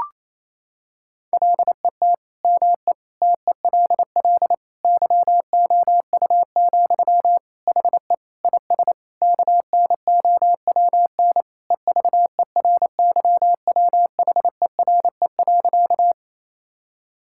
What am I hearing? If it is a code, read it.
Morse code, heard as a keyed tone, about 28 wpm, 700 Hz. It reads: LET ME TELL YOU, HE IS KNOWN EVERYWHERE.
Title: Sentences from Top 1600 Words 28wpm